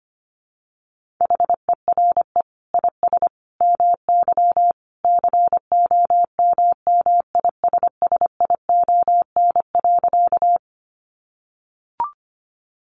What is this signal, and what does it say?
Morse code, heard as a keyed tone, about 25 wpm, 700 Hz. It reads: HERE IS MY COMMISSION.
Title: Sentences from Top 1400 Words 25wpm